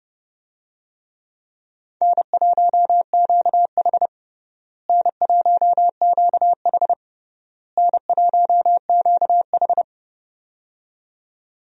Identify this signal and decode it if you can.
Morse code, heard as a keyed tone, about 30 wpm, 700 Hz. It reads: N1QH N1QH N1QH
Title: USA Call Signs - 4 Characters - Repeated 3x - 30wpm